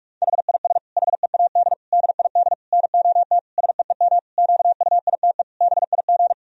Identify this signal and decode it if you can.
Morse code, heard as a keyed tone, about 46 wpm, 700 Hz. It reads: HIS HEAD DID NOT SEEM QUITE BIG
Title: Sentences from Top 1000 Words - Review Part 2 - 45wpm